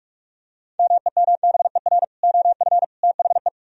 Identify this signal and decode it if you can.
Morse code, heard as a keyed tone, about 49 wpm, 700 Hz. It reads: MEMBER OF THE